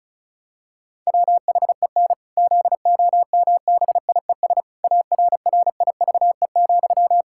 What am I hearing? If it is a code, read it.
Morse code, heard as a keyed tone, about 34 wpm, 700 Hz. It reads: WHEN ZOMBIES ARRIVE,
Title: Pangram Challenge v2 - 35wpm